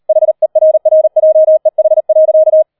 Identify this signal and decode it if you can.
Morse code, heard as a keyed tone, about 38 wpm, 630 Hz. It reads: HERRJEH.